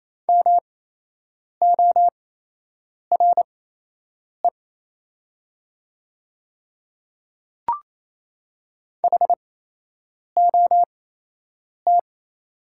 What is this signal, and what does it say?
Morse code, heard as a keyed tone, about 28 wpm, spaced as for 8 wpm, 700 Hz. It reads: MORE HOT
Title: 8x Character Spacing - Top 100 Words - 28wpm